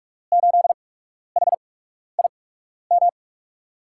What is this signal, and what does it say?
Morse code, heard as a keyed tone, about 44 wpm, spaced as for 12 wpm, 700 Hz. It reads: 8HIM
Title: USA Call Signs - 6 Characters - 8x Character Spacing - 45wpm